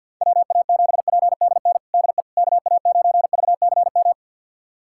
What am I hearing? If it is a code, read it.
Morse code, heard as a keyed tone, about 53 wpm, 700 Hz. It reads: WA7PDN DE KA9VCM
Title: USA Callsign DE Callsign 50wpm